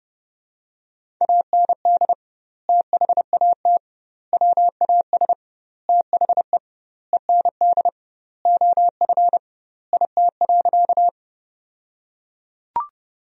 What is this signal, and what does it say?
Morse code, heard as a keyed tone, about 30 wpm, 700 Hz. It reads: AND THAT WAS THE END OF IT.
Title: 2x Word-Spacing - Sentences from Top 200 Words - 30wpm